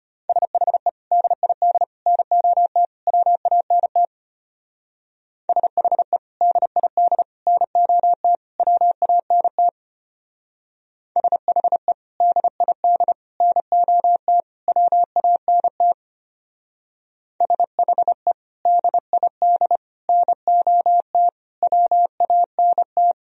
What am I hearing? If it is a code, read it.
Morse code, heard as a keyed tone, about 31 wpm, 700 Hz. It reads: SHE DID NOT WANT SHE DID NOT WANT SHE DID NOT WANT SHE DID NOT WANT
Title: Speed-Racing - Sets of 4 Words 25wpm